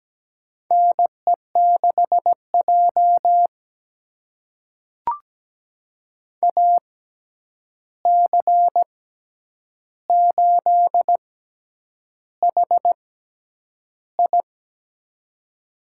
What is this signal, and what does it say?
Morse code, heard as a keyed tone, about 17 wpm, spaced as for 8 wpm, 700 Hz. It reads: NE6J AC8HI
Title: USA Call Signs - 6x Character Spacing - 17wpm